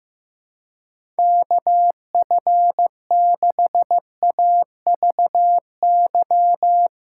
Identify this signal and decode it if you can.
Morse code, heard as a keyed tone, about 15 wpm, 700 Hz. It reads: KF6AVY
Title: USA Callsign DE Callsign - 6 Characters - 15wpm